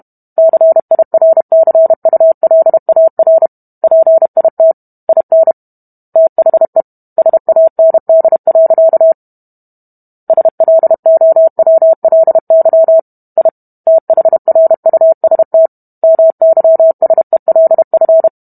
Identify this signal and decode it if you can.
Morse code, heard as a keyed tone, about 32 wpm, 650 Hz. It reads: CIRCULAR PIT IN THE SAND. SLOWLY I THRUST MYSELF